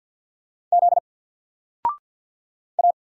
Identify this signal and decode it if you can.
Morse code, heard as a keyed tone, about 50 wpm, 700 Hz. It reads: Z A
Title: Single Letter-Number 50wpm